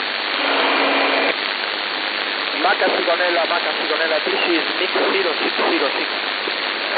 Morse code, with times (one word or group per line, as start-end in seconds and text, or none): none